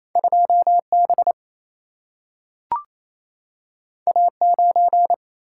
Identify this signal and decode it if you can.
Morse code, heard as a keyed tone, about 28 wpm, 700 Hz. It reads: 2 B A 9
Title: Single-Letter Single-Number 28wpm